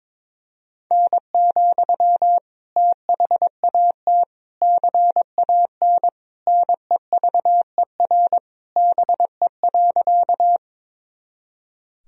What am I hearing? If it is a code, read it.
Morse code, heard as a keyed tone, about 22 wpm, 700 Hz. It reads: N, THAT CAN NEVER BE.
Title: Sentences from Top 300 Words and Abbreviations 22wpm